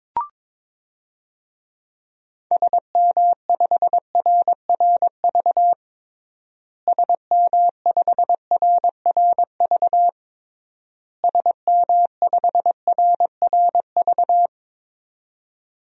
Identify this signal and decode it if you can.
Morse code, heard as a keyed tone, about 22 wpm, 700 Hz. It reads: SM5RRV SM5RRV SM5RRV
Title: DXCC-100 Call Signs - 6 Characters - Repeated 3x - 22wpm